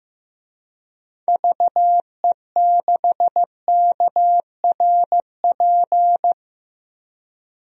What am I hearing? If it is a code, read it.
Morse code, heard as a keyed tone, about 15 wpm, 700 Hz. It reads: VE6KRP
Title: DXCC-100 Call Signs - 6 Characters - 15wpm